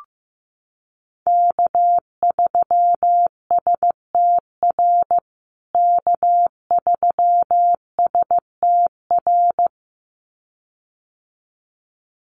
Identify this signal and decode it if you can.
Morse code, heard as a keyed tone, about 15 wpm, 700 Hz. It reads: K3STR K3STR
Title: USA Call Signs Repeated 2x 15wpm